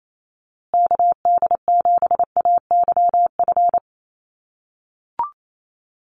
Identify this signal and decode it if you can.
Morse code, heard as a keyed tone, about 28 wpm, 700 Hz. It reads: KD7AYF
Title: USA Call Signs Repeated 3x 28wpm